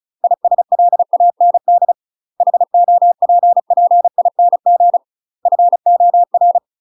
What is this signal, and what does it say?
Morse code, heard as a keyed tone, about 34 wpm, 700 Hz. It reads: ISLAND HOPPING FOR
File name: UK MASTER QSOs TRAINING/35WPM